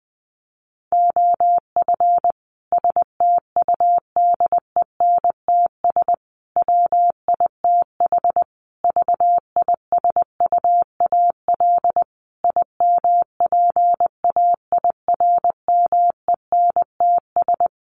Morse code, heard as a keyed tone, about 20 wpm, 700 Hz. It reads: OF STUDENTS WITH VISUAL IMPAIRMENTS
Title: Sets of 5 Words Encore 20wpm